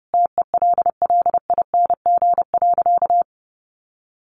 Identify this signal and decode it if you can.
Morse code, heard as a keyed tone, about 30 wpm, 700 Hz. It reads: TELLING.
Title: A Sound Of Thunder 30wpm